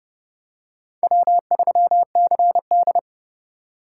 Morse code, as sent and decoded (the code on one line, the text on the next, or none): .-- ...-- -.-. -..
W3CD